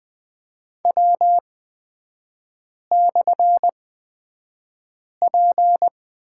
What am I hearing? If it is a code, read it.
Morse code, heard as a keyed tone, about 20 wpm, spaced as for 5 wpm, 690 Hz. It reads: W/P
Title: 229 Lesson 37e - Letter x - Three Characters 20wpm